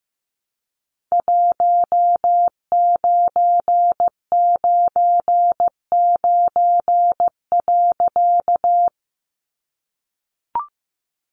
Morse code, heard as a keyed tone, about 15 wpm, 700 Hz. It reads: 1999.
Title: A Sound Of Thunder 15wpm